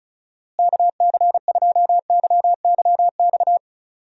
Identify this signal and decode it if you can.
Morse code, heard as a keyed tone, about 36 wpm, 700 Hz. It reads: KC2YYX
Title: USA Call Signs - 6 Characters - Repeated 2x - 35wpm